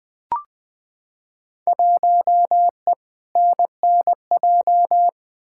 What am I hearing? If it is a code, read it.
Morse code, heard as a keyed tone, about 20 wpm, 700 Hz. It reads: 1E NNJ
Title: Field Day Exchanges - 20wpm